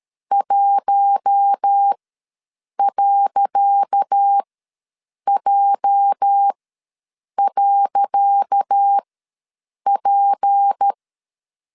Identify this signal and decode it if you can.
Morse code, heard as a keyed tone, about 13 wpm, spaced as for 7 wpm, 800 Hz. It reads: 1.J.P
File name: LEZIONI / 10 SIGNS / 10C DOT ws 12wpm 3548